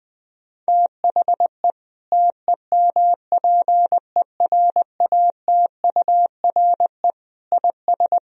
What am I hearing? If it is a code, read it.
Morse code, heard as a keyed tone, about 20 wpm, 700 Hz. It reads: THE TEMPERATURE IS
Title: Sentences from Top 1000 Words - Review Part 4 - 20wpm